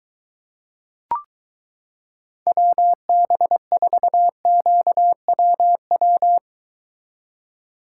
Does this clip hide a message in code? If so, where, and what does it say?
Yes: Morse code from 2.5 to 6.4 s, heard as a keyed tone, about 22 wpm, 700 Hz. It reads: WB4QWW